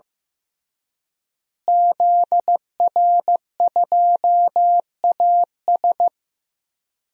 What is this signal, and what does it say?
Morse code, heard as a keyed tone, about 16 wpm, 700 Hz. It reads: ZR2AS
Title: DXCC-100 Call Signs - 5 Characters - Repeated 3x - 15wpm